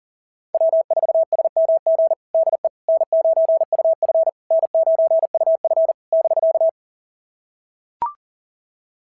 Morse code, heard as a keyed tone, about 38 wpm, 690 Hz. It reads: W4SMG DE N9UF N9UF <BK>
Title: SOTA QSO Practice 40wpm